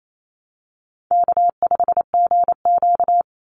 Morse code, heard as a keyed tone, about 28 wpm, 700 Hz. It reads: K5GQ
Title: Easy Call Signs - 28wpm